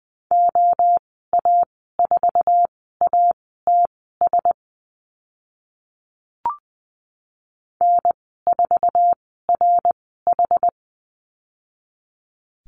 Morse code, heard as a keyed tone, about 20 wpm, spaced as for 14 wpm, 700 Hz. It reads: OA4ATS N4RH
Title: 062 Lesson 11g - Letter h - Call Signs 20wpm